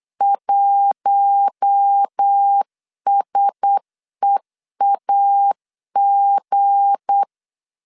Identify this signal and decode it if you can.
Morse code, heard as a keyed tone, about 8 wpm, 800 Hz. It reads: 1SEAG
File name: LEZIONI / 7 GF9Z / 7 G 8wpm 3118